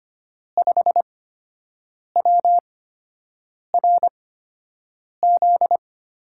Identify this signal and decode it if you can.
Morse code, heard as a keyed tone, about 25 wpm, spaced as for 7 wpm, 700 Hz. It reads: 5WRZ
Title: USA Call Signs - 5 Characters - 8x Character Spacing - 25wpm